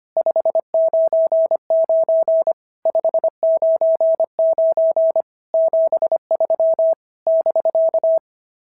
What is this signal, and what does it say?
Morse code, heard as a keyed tone, about 24 wpm, 650 Hz. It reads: 599 599 73 <BK>
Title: SOTA QSO Practice 25wpm